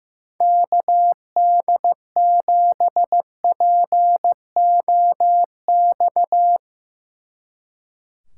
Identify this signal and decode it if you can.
Morse code, heard as a keyed tone, about 16 wpm, 700 Hz. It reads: KD7POX
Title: Speed Racing - USA 2x3 Call Signs 15wpm